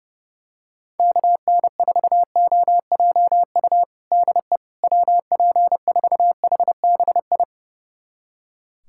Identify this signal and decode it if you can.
Morse code, heard as a keyed tone, about 32 wpm, 700 Hz. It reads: KN4OJU DE WP4HBI
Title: USA Callsign DE Callsign 30wpm